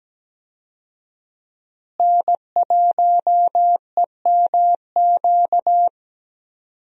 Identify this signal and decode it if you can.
Morse code, heard as a keyed tone, about 17 wpm, 700 Hz. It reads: N1EMQ
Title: USA Call Signs - 5 Characters - Repeated 2x - 17wpm